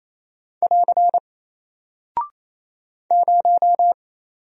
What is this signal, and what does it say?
Morse code, heard as a keyed tone, about 27 wpm, 700 Hz. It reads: <AR> 0
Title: Single Character (Letter:Number:Punctuation:Prosign) 28wpm